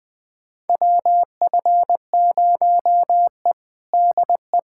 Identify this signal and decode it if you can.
Morse code, heard as a keyed tone, about 20 wpm, 700 Hz. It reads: WF0E DE
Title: USA Callsign DE Callsign - 4 Characters - 20wpm